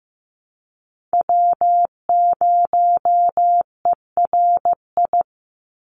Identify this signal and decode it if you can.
Morse code, heard as a keyed tone, about 15 wpm, 700 Hz. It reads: W0ERI
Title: NA SOTA Chaser Callsigns 15wpm